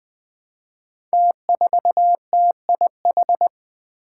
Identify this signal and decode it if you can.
Morse code, heard as a keyed tone, about 20 wpm, 700 Hz. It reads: T4TIH
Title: Five Challenging Letter-Number Combinations 20wpm